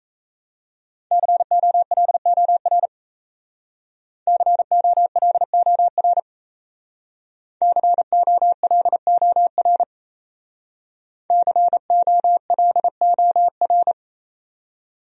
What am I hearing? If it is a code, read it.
Morse code, heard as a keyed tone, about 36 wpm, 700 Hz. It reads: COLOR COLOR COLOR COLOR